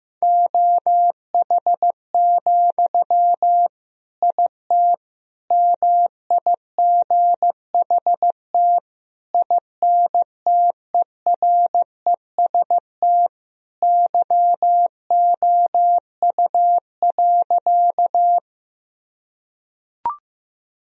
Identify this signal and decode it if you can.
Morse code, heard as a keyed tone, about 15 wpm, 700 Hz. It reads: OH, IT MIGHT INTEREST YOU.
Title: Sentences from Top 1000 Words - Review Part 2 - 15wpm